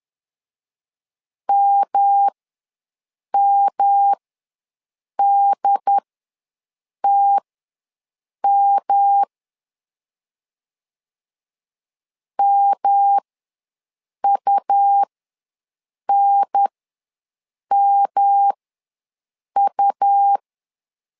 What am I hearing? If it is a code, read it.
Morse code, heard as a keyed tone, about 10 wpm, spaced as for 6 wpm, 800 Hz. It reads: MMDTM MUNMU
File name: LEZIONI / 3 MWRO / 3 M ws 10wpm 2121